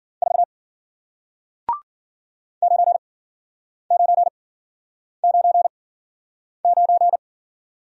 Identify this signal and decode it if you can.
Morse code, heard as a keyed tone, about 55 wpm, spaced as for 9 wpm, 700 Hz. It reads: 4 9999